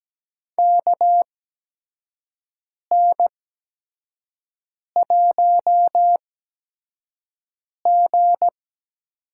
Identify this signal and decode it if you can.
Morse code, heard as a keyed tone, about 17 wpm, spaced as for 5 wpm, 700 Hz. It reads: KN1G
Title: USA Call Signs - 4 Characters - 8x Character Spacing - 17wpm